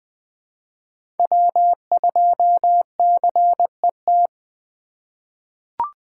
Morse code, heard as a keyed tone, about 20 wpm, 700 Hz. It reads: W2CET
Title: USA Call Signs - 5 Characters - 4x Character Spacing - 20wpm